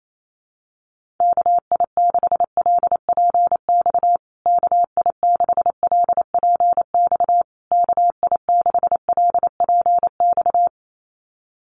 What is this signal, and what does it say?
Morse code, heard as a keyed tone, about 28 wpm, 700 Hz. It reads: KI6LPX KI6LPX KI6LPX
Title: USA Call Signs Repeated 3x 28wpm